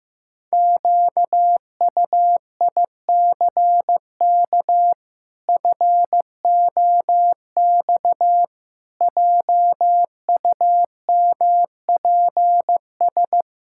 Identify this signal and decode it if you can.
Morse code, heard as a keyed tone, about 15 wpm, 700 Hz. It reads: QUICK FOX JUMPS
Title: Pangram Challenge v2 - 15wpm